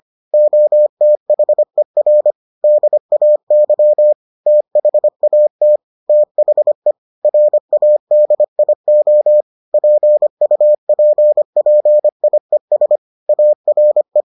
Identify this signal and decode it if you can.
Morse code, heard as a keyed tone, about 25 wpm, 600 Hz. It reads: OTHER DAY THAT THE RADIO PUPPIES ARE